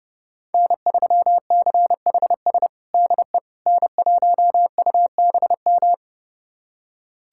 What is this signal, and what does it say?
Morse code, heard as a keyed tone, about 32 wpm, 700 Hz. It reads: N3CHS DE N1UBM
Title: USA Callsign DE Callsign - 5 Characters - 30wpm